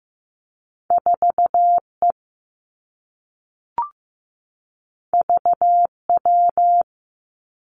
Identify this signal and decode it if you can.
Morse code, heard as a keyed tone, about 15 wpm, 700 Hz. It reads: 4 E V W
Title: Two Challenging Letter-Number Combinations 15wpm